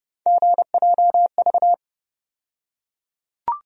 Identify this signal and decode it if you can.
Morse code, heard as a keyed tone, about 32 wpm, 700 Hz. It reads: GJV
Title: USA Call Sign Postfixes - 3 Letters 30wpm